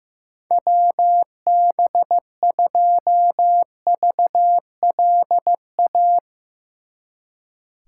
Morse code, heard as a keyed tone, about 16 wpm, 700 Hz. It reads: WB2VLA
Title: Speed Racing - USA 2x3 Call Signs 15wpm